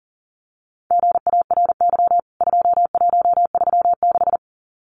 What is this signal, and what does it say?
Morse code, heard as a keyed tone, about 40 wpm, 700 Hz. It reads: GARY 2136
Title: CWT Exchange 40wpm